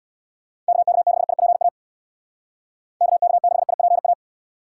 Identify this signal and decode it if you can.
Morse code, heard as a keyed tone, about 66 wpm, 700 Hz. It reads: KK6IPA KK6IPA